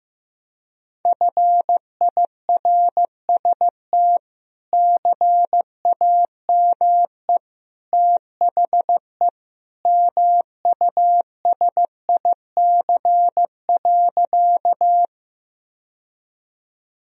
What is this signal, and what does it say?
Morse code, heard as a keyed tone, about 16 wpm, 700 Hz. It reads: FIRST CAME THE MUSIC.
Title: Sentences from Top 1000 Words - Review Part 1 - 15wpm